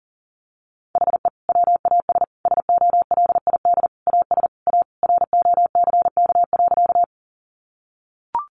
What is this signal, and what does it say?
Morse code, heard as a keyed tone, about 40 wpm, 700 Hz. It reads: HE WAS SOLID AS A ROCK.
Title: Sentences from Top 1500 Words 40wpm